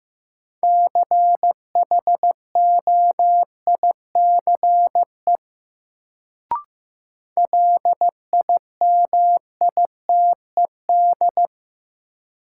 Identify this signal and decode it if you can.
Morse code, heard as a keyed tone, about 15 wpm, 700 Hz. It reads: CHOICE LIMITED
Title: Sentences from Top 900 Words - New Word Drill - 15wpm